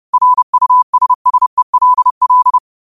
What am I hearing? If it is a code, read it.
Morse code, heard as a keyed tone, about 30 wpm, 1000 Hz. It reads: RUSSELL